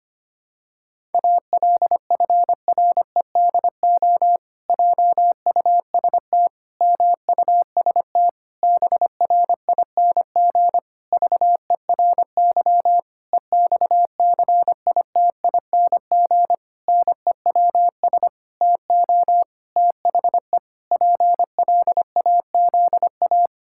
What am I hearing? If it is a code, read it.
Morse code, heard as a keyed tone, about 25 wpm, 700 Hz. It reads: ALFREDO JUST MUST BRING VERY EXCITING NEWS TO THE PLAZA
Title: Pangram Challenge v2 - 25wpm